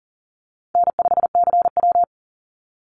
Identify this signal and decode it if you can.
Morse code, heard as a keyed tone, about 40 wpm, 700 Hz. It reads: N5CW
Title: Easy Call Signs Repeated 3x - 40wpm